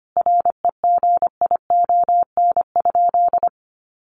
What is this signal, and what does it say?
Morse code, heard as a keyed tone, about 25 wpm, 700 Hz. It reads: REGION?
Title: Sentences from Top 1500 Words 25wpm